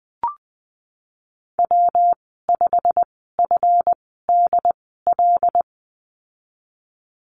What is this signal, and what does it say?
Morse code, heard as a keyed tone, about 20 wpm, spaced as for 14 wpm, 700 Hz. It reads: W5FDL
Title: 167 Lesson 27g - Number 7 - Call Signs 20wpm